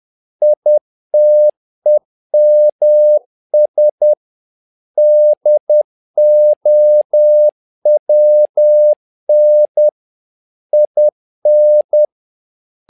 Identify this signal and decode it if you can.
Morse code, heard as a keyed tone, about 10 wpm, 600 Hz. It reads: ITEMS DOWN IN
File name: learning center / tower trouble / tower trouble 10wpm